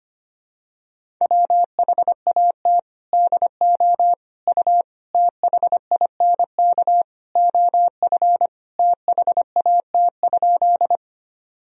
Morse code, heard as a keyed tone, about 25 wpm, 700 Hz. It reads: WHAT DO U THINK OF THAT?
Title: Sentences from Top 200 Words and Abbreviations 25wpm